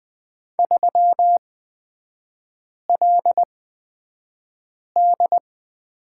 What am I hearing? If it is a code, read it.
Morse code, heard as a keyed tone, about 20 wpm, spaced as for 5 wpm, 700 Hz. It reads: 3LD
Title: 250 Lesson 40e - BK - Three Characters 20wpm